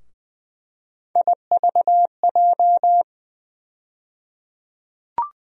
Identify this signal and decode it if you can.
Morse code, heard as a keyed tone, about 20 wpm, 700 Hz. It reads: IVJ
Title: USA Call Sign Postfixes - 3 Letters 20wpm